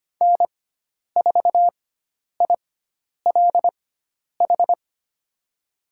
Morse code, heard as a keyed tone, about 25 wpm, spaced as for 10 wpm, 700 Hz. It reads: N4ILH
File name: Lesson 16g - Letter c - Call Signs 25wpm